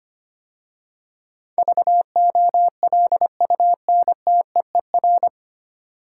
Morse code, heard as a keyed tone, about 25 wpm, 700 Hz. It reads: VOLUNTEER
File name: practice / 25wpm / 9 Letter Words - First 3 Letters Given - 25wpm